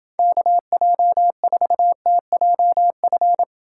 Morse code, heard as a keyed tone, about 30 wpm, 700 Hz. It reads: KJ4TJF